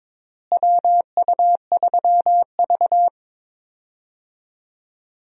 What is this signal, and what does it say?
Morse code, heard as a keyed tone, about 22 wpm, 700 Hz. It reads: WU3V
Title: USA Call Signs - 4 Characters - Repeated 3x - 22wpm